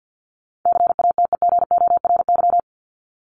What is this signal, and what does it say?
Morse code, heard as a keyed tone, about 50 wpm, 700 Hz. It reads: CATEGORY
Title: Top 751-1000 Words 50wpm